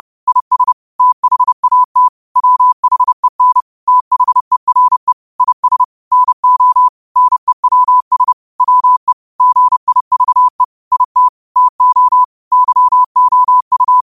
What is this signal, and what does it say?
Morse code, heard as a keyed tone, about 30 wpm, 1000 Hz. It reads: IS THAT WHEN THERE IS NO NEWS WE GIVE IT TO YOU